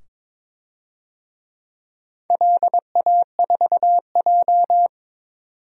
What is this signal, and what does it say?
Morse code, heard as a keyed tone, about 22 wpm, 690 Hz. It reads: LA4J
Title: DXCC-100 Call Signs - 4 Characters - 22wpm